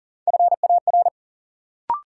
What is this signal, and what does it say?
Morse code, heard as a keyed tone, about 40 wpm, 700 Hz. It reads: FAR
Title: Sentences from Top 300 Words - New Word Drill - 40wpm